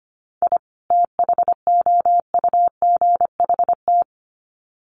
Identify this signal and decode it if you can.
Morse code, heard as a keyed tone, about 25 wpm, 700 Hz. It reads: I THOUGHT
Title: Sets of 2 Words 25wpm